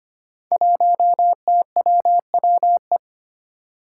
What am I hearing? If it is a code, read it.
Morse code, heard as a keyed tone, about 25 wpm, 700 Hz. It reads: 1TWWE
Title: Five Challenging Letter-Number Combinations 25wpm